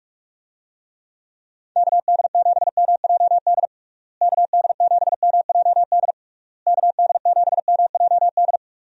Morse code, heard as a keyed tone, about 41 wpm, 700 Hz. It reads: KD7MJD KD7MJD KD7MJD
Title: USA Call Signs - 6 Characters - Repeated 3x - 45wpm